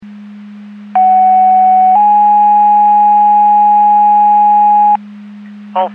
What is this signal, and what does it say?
Two-tone page, heard as a one-second tone, then a three-second tone, about 750 then 830 Hz. It is heard from 1.0 to 5.0 s.
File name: Portage County 154.3850 Fire EMS Paging